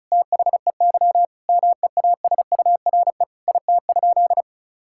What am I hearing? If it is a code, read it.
Morse code, heard as a keyed tone, about 36 wpm, 700 Hz. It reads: THEY MEASURE IT?
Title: Sentences from Top 1000 Words - Review Part 6 - 35wpm